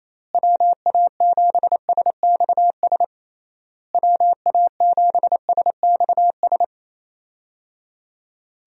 Morse code, heard as a keyed tone, about 28 wpm, 700 Hz. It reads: WA7SXS WA7SXS
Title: USA Call Signs - 6 Characters - Repeated 3x - 28wpm